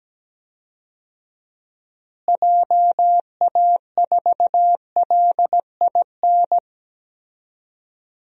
Morse code, heard as a keyed tone, about 17 wpm, 700 Hz. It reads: JA4LIN
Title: DXCC-100 Call Signs - 6 Characters - Repeated 2x - 17wpm